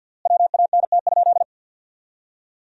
Morse code, heard as a keyed tone, about 50 wpm, 700 Hz. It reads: WANT?
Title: 3x Word-Spacing - Sentences from Top 200 Words - 50wpm